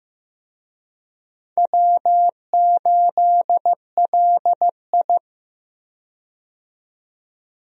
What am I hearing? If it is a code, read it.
Morse code, heard as a keyed tone, about 15 wpm, 700 Hz. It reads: W8LI
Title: USA Call Signs - 4 Characters - 15wpm